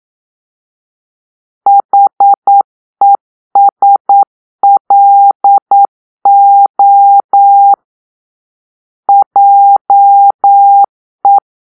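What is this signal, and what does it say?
Morse code, heard as a keyed tone, about 9 wpm, 800 Hz. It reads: HESLO JE
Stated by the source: Morse-code cipher made by a player team for the Kachna puzzle game.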